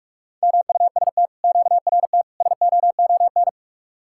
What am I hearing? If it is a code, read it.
Morse code, heard as a keyed tone, about 48 wpm, 700 Hz. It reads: MUST QRT SOON